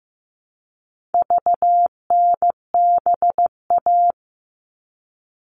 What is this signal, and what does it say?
Morse code, heard as a keyed tone, about 15 wpm, 700 Hz. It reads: VNBA
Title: All Cut Numbers - 4-Digit 15wpm